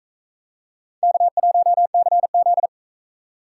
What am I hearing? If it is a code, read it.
Morse code, heard as a keyed tone, about 40 wpm, 700 Hz. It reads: K1CZ